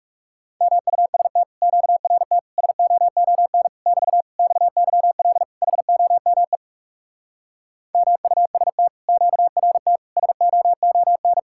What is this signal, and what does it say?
Morse code, heard as a keyed tone, about 40 wpm, 700 Hz. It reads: MUST QRT SOON <BT> XYL HOME MUST QRT SOON